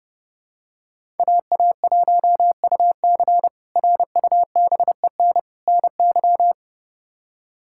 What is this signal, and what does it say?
Morse code, heard as a keyed tone, about 30 wpm, 700 Hz. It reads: AA1UC RUBEN NY
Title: K1USN SST QSO Practice - 30wpm